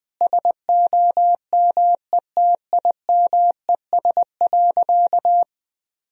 Morse code, heard as a keyed tone, about 19 wpm, 700 Hz. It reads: SOMETIMES.
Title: Sentences from Top 1000 Words - Review Part 3 - 20wpm